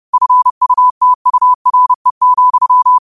Morse code, heard as a keyed tone, about 30 wpm, 1000 Hz. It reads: FUTURE,